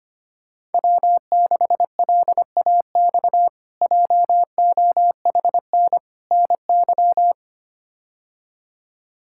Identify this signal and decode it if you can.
Morse code, heard as a keyed tone, about 25 wpm, 700 Hz. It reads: W6LAX JOHN NY
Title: K1USN SST QSO Practice - 25wpm